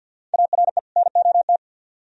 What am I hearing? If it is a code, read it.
Morse code, heard as a keyed tone, about 67 wpm, spaced as for 58 wpm, 700 Hz. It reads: ARE NOT
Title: Speed-Racing - Sets of 2 Words 50wpm